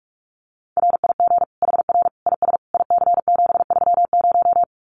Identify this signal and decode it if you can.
Morse code, heard as a keyed tone, about 44 wpm, 700 Hz. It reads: RIG HR IS IC730
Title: Common QSO Phrases 45wpm